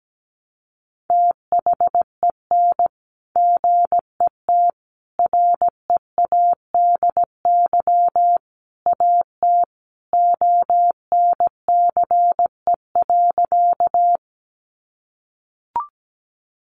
Morse code, heard as a keyed tone, about 17 wpm, 700 Hz. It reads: THEN GET READY AT ONCE.
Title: Sentences from Top 400 Words 17wpm